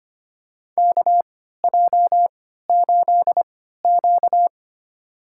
Morse code, heard as a keyed tone, about 25 wpm, spaced as for 14 wpm, 700 Hz. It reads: KJ8Q
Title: USA Call Signs - 3x Character Spacing - 25wpm